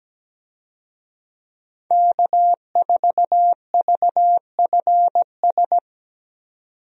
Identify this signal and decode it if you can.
Morse code, heard as a keyed tone, about 17 wpm, 700 Hz. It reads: K4VFS
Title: USA Call Signs - 5 Characters - 17wpm